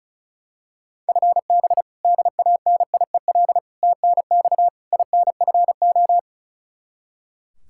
Morse code, heard as a keyed tone, about 34 wpm, 700 Hz. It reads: FB DANIEL TNX INFO
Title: Practice QSO - Exchange 3 - More Info - 35wpm